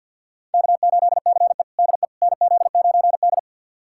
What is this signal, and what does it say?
Morse code, heard as a keyed tone, about 48 wpm, 700 Hz. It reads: K8CE DE NZ9D
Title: USA Callsign DE Callsign - 4 Characters - 50wpm